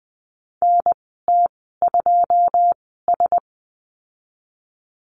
Morse code, heard as a keyed tone, about 20 wpm, spaced as for 15 wpm, 700 Hz. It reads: NT2S
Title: 161 Lesson 26g - Letter g - Call Signs 20wpm